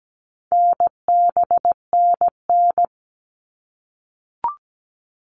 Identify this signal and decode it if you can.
Morse code, heard as a keyed tone, about 17 wpm, 700 Hz. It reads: NBNN
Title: All Cut Numbers - 4-Digit 17wpm